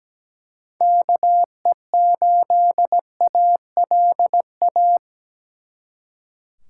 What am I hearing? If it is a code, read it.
Morse code, heard as a keyed tone, about 20 wpm, 700 Hz. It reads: KE8ALA
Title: Speed Racing - USA 2x3 Call Signs 17wpm